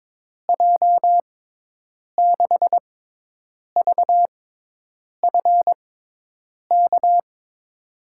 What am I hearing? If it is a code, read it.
Morse code, heard as a keyed tone, about 22 wpm, spaced as for 8 wpm, 700 Hz. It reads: J6VFK
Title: USA Call Signs - 6 Characters - 6x Character Spacing - 22wpm